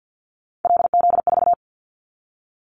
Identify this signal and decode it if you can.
Morse code, heard as a keyed tone, about 50 wpm, 700 Hz. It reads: LZ4
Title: Callsign Prefixes for Top 50 DXCC - 50wpm